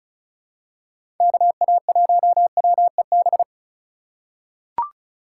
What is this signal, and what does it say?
Morse code, heard as a keyed tone, about 34 wpm, 700 Hz. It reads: KA1WEB
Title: USA Call Signs - 6 Characters - 4x Character Spacing - 35wpm